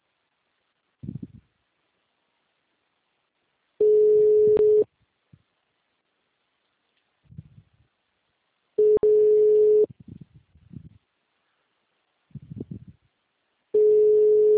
Ringback tone (Europe, 425 Hz).
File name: out- -2011- - - .68624